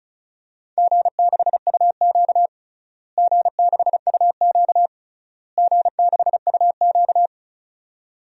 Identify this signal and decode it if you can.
Morse code, heard as a keyed tone, about 34 wpm, 700 Hz. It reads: G6UQ G6UQ G6UQ
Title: DXCC-100 Call Signs - 4 Characters - Repeated 3x - 35wpm